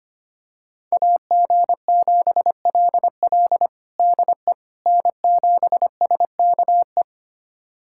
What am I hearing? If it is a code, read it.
Morse code, heard as a keyed tone, about 25 wpm, 700 Hz. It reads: AG7LL DE N7SKE
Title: USA Callsign DE Callsign - 5 Characters - 25wpm